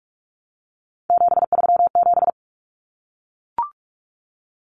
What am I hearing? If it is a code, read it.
Morse code, heard as a keyed tone, about 44 wpm, 700 Hz. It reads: 737
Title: Common Cut Numbers - 3-Digit 45wpm